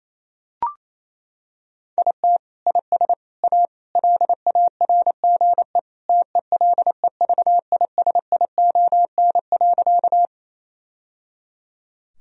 Morse code, heard as a keyed tone, about 28 wpm, 700 Hz. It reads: IT IS A LARGE TELEVISION.
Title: Sentences from Top 1000 Words - Review Part 4 - 28wpm